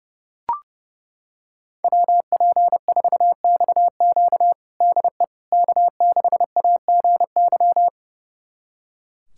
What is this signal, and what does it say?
Morse code, heard as a keyed tone, about 32 wpm, 700 Hz. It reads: WP4XQ DE K6AGY
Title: USA Callsign DE Callsign 30wpm